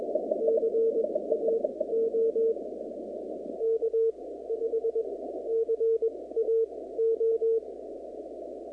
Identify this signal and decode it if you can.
Morse code, heard as a keyed tone, about 21 wpm, spaced as for 15 wpm, 450 Hz. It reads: CAO K5CAO
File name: contests / 2019-cw-wpx-kt5j-k5pi- -2330 0-37